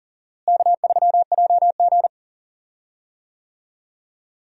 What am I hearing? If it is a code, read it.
Morse code, heard as a keyed tone, about 38 wpm, 700 Hz. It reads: K3JG
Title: USA Call Signs - 4 Characters - Repeated 3x - 40wpm